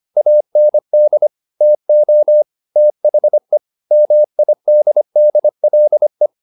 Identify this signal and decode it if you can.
Morse code, heard as a keyed tone, about 25 wpm, 600 Hz. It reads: AND TO THE MIDDLE